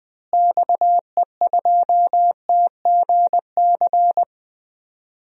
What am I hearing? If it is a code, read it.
Morse code, heard as a keyed tone, about 20 wpm, 700 Hz. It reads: XE2TGC
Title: DXCC-100 Call Signs - 6 Characters - Repeated 3x - 20wpm